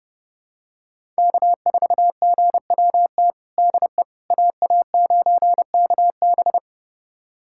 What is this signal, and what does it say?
Morse code, heard as a keyed tone, about 30 wpm, 700 Hz. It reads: K4GWT DE AA9KB
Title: USA Callsign DE Callsign 30wpm